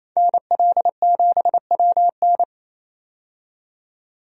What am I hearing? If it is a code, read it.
Morse code, heard as a keyed tone, about 28 wpm, 700 Hz. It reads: NL7WN
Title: USA Call Signs - 5 Characters - 28wpm